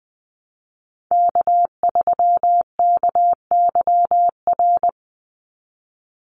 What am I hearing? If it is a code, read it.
Morse code, heard as a keyed tone, about 20 wpm, 700 Hz. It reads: K3KYR
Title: NA SOTA Chaser Callsigns 20wpm